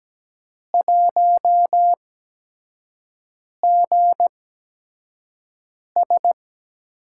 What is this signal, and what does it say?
Morse code, heard as a keyed tone, about 17 wpm, spaced as for 5 wpm, 700 Hz. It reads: 1GS
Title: USA Call Signs - 8x Character Spacing - 17wpm